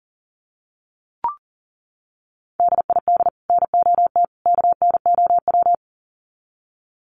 Morse code, heard as a keyed tone, about 40 wpm, 700 Hz. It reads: DID NOT KNOW
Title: Sets of 3 Words 40wpm